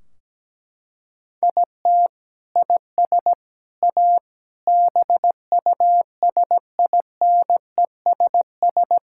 Morse code, heard as a keyed tone, about 17 wpm, 700 Hz. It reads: IT IS A BUSINESS
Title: Sentences from Top 1600 Words 17wpm